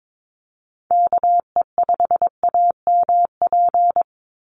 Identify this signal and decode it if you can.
Morse code, heard as a keyed tone, about 22 wpm, 700 Hz. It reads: KE5AMP
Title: USA Call Signs 22wpm